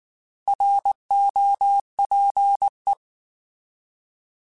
Morse code, heard as a keyed tone, about 19 wpm, 800 Hz. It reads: ROPE
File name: Alphabetical / Alphabetical mp3s / Rs @18